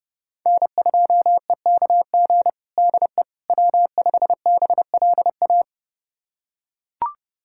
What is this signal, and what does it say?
Morse code, heard as a keyed tone, about 30 wpm, 700 Hz. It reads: N2EKG DE W5BLA
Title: USA Callsign DE Callsign - 5 Characters - 30wpm